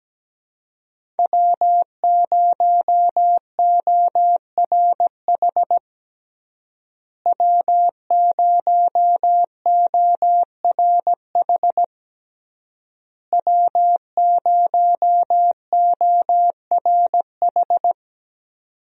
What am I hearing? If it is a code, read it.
Morse code, heard as a keyed tone, about 17 wpm, 700 Hz. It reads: W0ORH W0ORH W0ORH
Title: USA Call Signs - 5 Characters - Repeated 3x - 17wpm